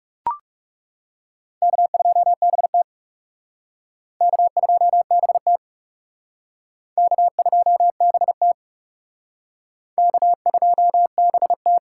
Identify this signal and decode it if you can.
Morse code, heard as a keyed tone, about 38 wpm, 700 Hz. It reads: K2BT K2BT K2BT K2BT